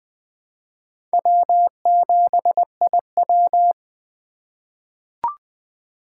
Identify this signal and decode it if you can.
Morse code, heard as a keyed tone, about 20 wpm, 700 Hz. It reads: W7IW
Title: USA Call Signs - 4 Characters - 2x Character Spacing - 20wpm